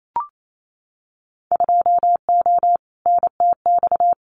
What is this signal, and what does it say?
Morse code, heard as a keyed tone, about 28 wpm, 700 Hz. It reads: 2O NTX
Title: Winter Field Day Exchanges Encore 28wpm